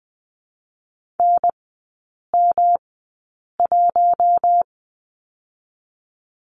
Morse code, heard as a keyed tone, about 20 wpm, spaced as for 8 wpm, 700 Hz. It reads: NM1
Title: 107 Lesson 18e - Letter m - Three Characters 20wpm